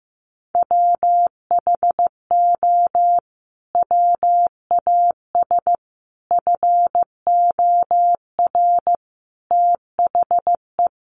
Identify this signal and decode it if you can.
Morse code, heard as a keyed tone, about 14 wpm, 700 Hz. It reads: WHO WAS FOR THE
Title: Sentences from Top 1400 Words 15wpm